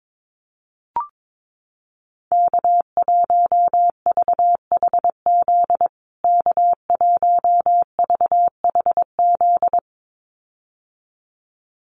Morse code, heard as a keyed tone, about 22 wpm, 700 Hz. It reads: K1VHZ K1VHZ
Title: USA Call Signs Repeated 2x 22wpm